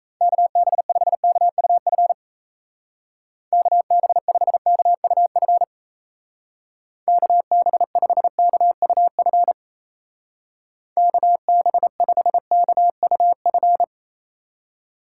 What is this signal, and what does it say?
Morse code, heard as a keyed tone, about 37 wpm, 700 Hz. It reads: KB5KUF KB5KUF KB5KUF KB5KUF